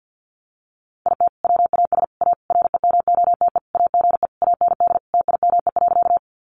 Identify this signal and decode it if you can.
Morse code, heard as a keyed tone, about 50 wpm, 700 Hz. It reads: IT WAS A REMOTE AGE AND TIME.
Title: Sentences from Top 1300 Words 50wpm